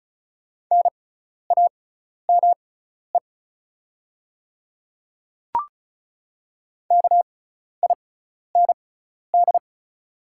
Morse code, heard as a keyed tone, about 36 wpm, spaced as for 12 wpm, 700 Hz. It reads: NAME KIND
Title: 6x Character Spacing - Top 300 Words - 35wpm